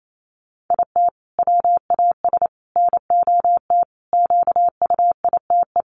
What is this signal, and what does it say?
Morse code, heard as a keyed tone, about 28 wpm, 700 Hz. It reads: IT WAS NOT QUITE
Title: Sentences from Top 700 Words 28wpm